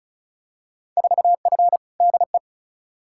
Morse code, heard as a keyed tone, about 36 wpm, 700 Hz. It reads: 4F DE
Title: Field Day Exchanges Encore - 35wpm